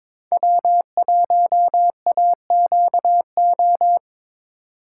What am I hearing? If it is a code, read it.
Morse code, heard as a keyed tone, about 26 wpm, 700 Hz. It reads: W1AQO
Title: Speed Racing - USA 1x3 Call Signs 22wpm